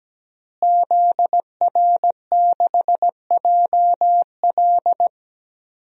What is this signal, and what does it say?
Morse code, heard as a keyed tone, about 17 wpm, 700 Hz. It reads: ZR6JL
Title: DXCC-100 Call Signs - 5 Characters - Repeated 2x - 17wpm